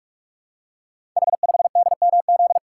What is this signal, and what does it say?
Morse code, heard as a keyed tone, about 44 wpm, 700 Hz. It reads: H5DMZ
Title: Code Groups - 5 Letter:Numbers - 45wpm